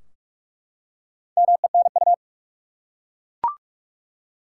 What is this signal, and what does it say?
Morse code, heard as a keyed tone, about 46 wpm, 700 Hz. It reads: MENU
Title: Four Letter Words - 45wpm